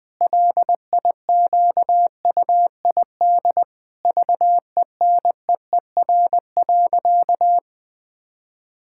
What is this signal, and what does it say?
Morse code, heard as a keyed tone, about 21 wpm, 700 Hz. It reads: LIQUID VENEER.
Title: Pangram Challenge v2 - 20wpm